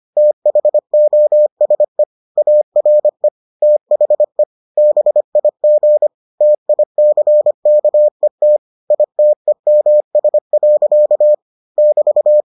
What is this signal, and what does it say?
Morse code, heard as a keyed tone, about 25 wpm, 600 Hz. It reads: THOSE ARE THE BIG TICKET ITEMS. <BT>